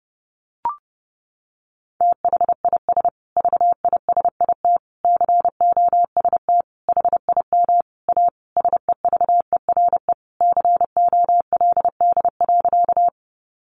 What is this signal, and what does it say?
Morse code, heard as a keyed tone, about 30 wpm, 700 Hz. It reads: THIS VISIT COST HIM A SEVERE COLD.
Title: Sentences from Top 1400 Words 30wpm